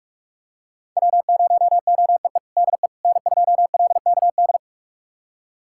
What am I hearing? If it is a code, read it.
Morse code, heard as a keyed tone, about 46 wpm, 700 Hz. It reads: W0OEE DE N2LKD
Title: USA Callsign DE Callsign - 5 Characters - 45wpm